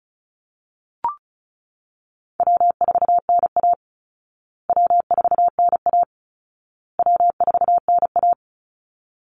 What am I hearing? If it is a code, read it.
Morse code, heard as a keyed tone, about 34 wpm, 700 Hz. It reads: W4NA W4NA W4NA
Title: Call Signs Repeated 2x - 35wpm